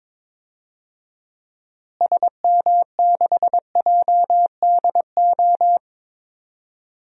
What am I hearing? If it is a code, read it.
Morse code, heard as a keyed tone, about 21 wpm, 700 Hz. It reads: SM6JDO
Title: DXCC-100 Call Signs - 6 Characters - Repeated 3x - 22wpm